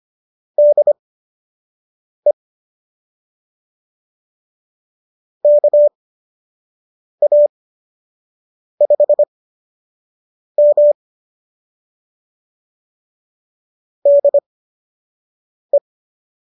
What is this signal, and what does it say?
Morse code, heard as a keyed tone, about 25 wpm, spaced as for 6 wpm, 590 Hz. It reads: DE KA5M DE